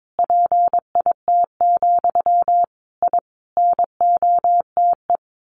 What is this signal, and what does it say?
Morse code, heard as a keyed tone, about 22 wpm, 700 Hz. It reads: PIT, I NOTE
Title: A Sound Of Thunder 22wpm